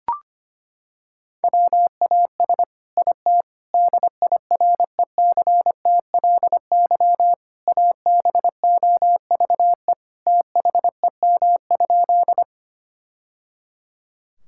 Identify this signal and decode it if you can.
Morse code, heard as a keyed tone, about 25 wpm, 700 Hz. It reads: WAS IT DIRECTLY ABOVE THEM?
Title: Sentences from Top 1000 Words - Review Part 5 - 25wpm